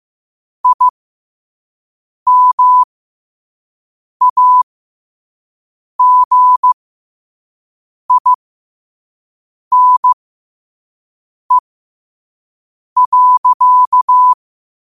Morse code, heard as a keyed tone, about 15 wpm, spaced as for 5 wpm, 1000 Hz. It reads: IMAGINE.
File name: qotdfiles / 5wpm